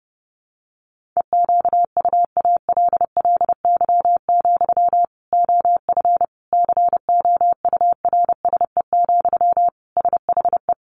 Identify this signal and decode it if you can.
Morse code, heard as a keyed tone, about 28 wpm, 700 Hz. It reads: EQUALLY, OF COURSE, SHE
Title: Sentences from Top 1400 Words 30wpm